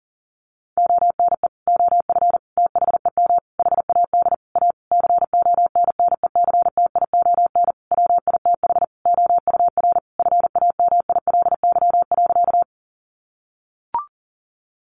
Morse code, heard as a keyed tone, about 40 wpm, 700 Hz. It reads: ONE OF THEM HAD A CONNECTION WITH OUR FAMILY.
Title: Sentences from Top 1300 Words 40wpm